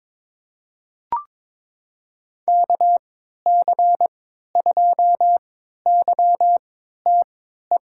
Morse code, heard as a keyed tone, about 22 wpm, spaced as for 12 wpm, 700 Hz. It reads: KC2YTE
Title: USA Call Signs - 6 Characters - 3x Character Spacing - 22wpm